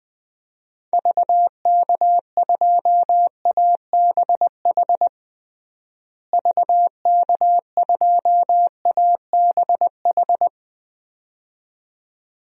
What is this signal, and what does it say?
Morse code, heard as a keyed tone, about 20 wpm, 700 Hz. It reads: VK2ABH VK2ABH
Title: DXCC-100 Call Signs - 6 Characters - Repeated 2x - 20wpm